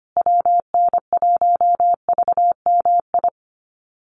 Morse code, heard as a keyed tone, about 25 wpm, 700 Hz. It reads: WN1VMI
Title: USA Call Signs Repeated 3x 25wpm